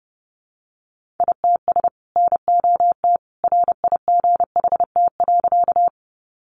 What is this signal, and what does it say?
Morse code, heard as a keyed tone, about 30 wpm, 700 Hz. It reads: ITS NOT RIGHT.
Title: Sentences from Top 600 Words 30wpm